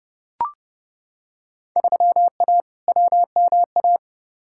Morse code, heard as a keyed tone, about 30 wpm, 700 Hz. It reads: 3A WMA
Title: Field Day Exchanges Encore - 30wpm